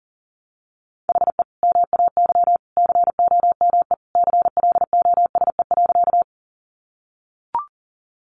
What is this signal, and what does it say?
Morse code, heard as a keyed tone, about 40 wpm, 700 Hz. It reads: HE MAY COME CLOSE.
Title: Sentences from Top 300 Words 40wpm